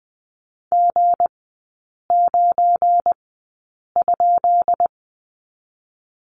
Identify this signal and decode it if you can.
Morse code, heard as a keyed tone, about 20 wpm, 700 Hz. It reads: G 9 ?
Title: 212 Lesson 34e - Number 8 - Three Characters 20wpm